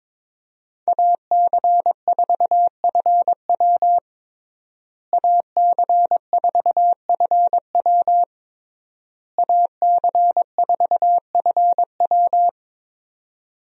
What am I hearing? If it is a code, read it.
Morse code, heard as a keyed tone, about 22 wpm, 700 Hz. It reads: AC4FW AC4FW AC4FW
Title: USA Call Signs - 5 Characters - Repeated 3x - 22wpm